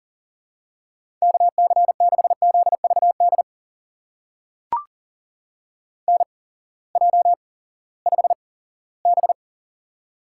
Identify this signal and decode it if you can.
Morse code, heard as a keyed tone, about 40 wpm, spaced as for 20 wpm, 700 Hz. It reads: KC6ZVD NJ5B
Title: USA Call Signs - 8x Character Spacing - 40wpm